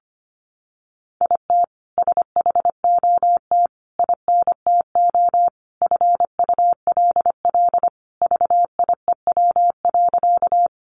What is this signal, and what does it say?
Morse code, heard as a keyed tone, about 25 wpm, 700 Hz. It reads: IT SHOT INTO FULL VIEW.
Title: Sentences from Top 1100 Words 25wpm